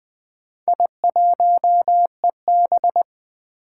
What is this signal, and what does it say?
Morse code, heard as a keyed tone, about 20 wpm, 700 Hz. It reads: I1EB
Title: Four Challenging Letter-Number Combinations 20wpm